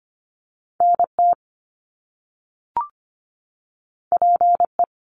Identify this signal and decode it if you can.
Morse code, heard as a keyed tone, about 25 wpm, 700 Hz. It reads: NT PE
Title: Canadian Provinces and Territories 25wpm